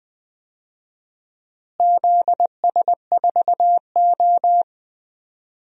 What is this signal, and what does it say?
Morse code, heard as a keyed tone, about 20 wpm, 700 Hz. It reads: ZS4O
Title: DXCC-100 Call Signs - 4 Characters - Repeated 3x - 20wpm